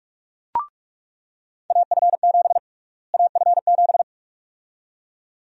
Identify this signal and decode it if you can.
Morse code, heard as a keyed tone, about 46 wpm, 700 Hz. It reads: AF7 AF7
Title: USA Call Sign Prefixes - Repeated 2x - 45wpm